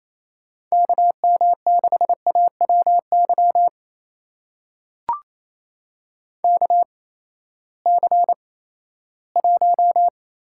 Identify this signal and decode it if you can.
Morse code, heard as a keyed tone, about 28 wpm, spaced as for 16 wpm, 700 Hz. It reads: KM6AWY KC1
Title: USA Call Signs - 6 Characters - 8x Character Spacing - 28wpm